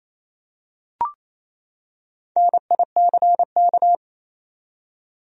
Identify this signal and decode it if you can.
Morse code, heard as a keyed tone, about 28 wpm, 700 Hz. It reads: NICK
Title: USA Common Names - 28wpm